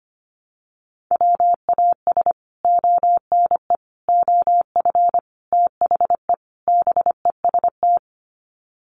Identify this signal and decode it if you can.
Morse code, heard as a keyed tone, about 25 wpm, 700 Hz. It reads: WAS ONE OF THE BEST
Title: Sets of 5 Words Encore2 25wpm